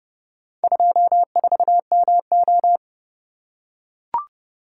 Morse code, heard as a keyed tone, about 29 wpm, 700 Hz. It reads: 24MO